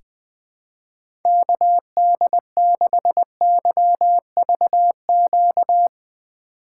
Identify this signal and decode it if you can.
Morse code, heard as a keyed tone, about 21 wpm, 700 Hz. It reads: KD6YVQ
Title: USA Call Signs - 6 Characters - 4x Character Spacing - 20wpm